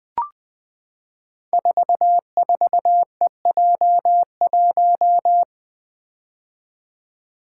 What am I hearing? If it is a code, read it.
Morse code, heard as a keyed tone, about 20 wpm, 700 Hz. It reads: 44EJ1
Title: Code Groups - 5 Letter:Numbers - 20wpm